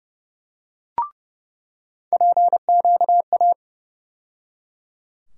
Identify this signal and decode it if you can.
Morse code, heard as a keyed tone, about 31 wpm, 700 Hz. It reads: PQA
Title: USA Call Sign Postfixes - 3 Letters 30wpm